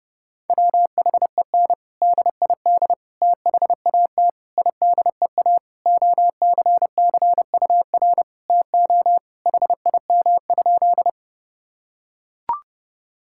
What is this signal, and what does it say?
Morse code, heard as a keyed tone, about 30 wpm, 700 Hz. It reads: WHEN DID THAT IDEA OCCUR TO HIM?
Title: Sentences from Top 1000 Words - Review Part 6 - 30wpm